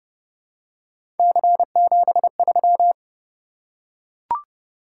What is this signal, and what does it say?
Morse code, heard as a keyed tone, about 29 wpm, 700 Hz. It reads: C73